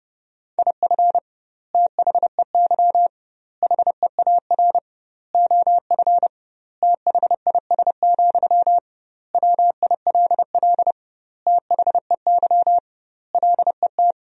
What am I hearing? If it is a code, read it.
Morse code, heard as a keyed tone, about 30 wpm, 700 Hz. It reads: IF THEY HEAR OF THIS, WILL THEY LET
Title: 2x Word-Spacing - Sentences from Top 400 Words - 30wpm